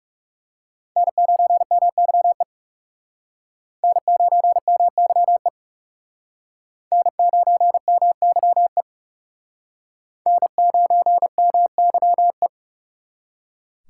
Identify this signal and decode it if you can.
Morse code, heard as a keyed tone, about 36 wpm, 700 Hz. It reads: N9MYE N9MYE N9MYE N9MYE
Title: Speed Racing - USA 1x3 Call Signs 30wpm